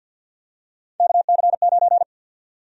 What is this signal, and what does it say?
Morse code, heard as a keyed tone, about 49 wpm, 700 Hz. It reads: KC9
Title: USA Call Sign Prefixes - 50wpm